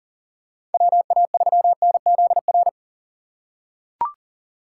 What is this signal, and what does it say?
Morse code, heard as a keyed tone, about 40 wpm, 700 Hz. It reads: WA3NZR
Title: K1USN SST QSO Practice - 40wpm